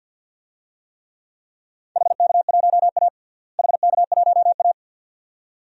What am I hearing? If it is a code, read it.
Morse code, heard as a keyed tone, about 50 wpm, 700 Hz. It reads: HK1A HK1A
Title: DXCC-100 Call Signs - 4 Characters - Repeated 2x - 50wpm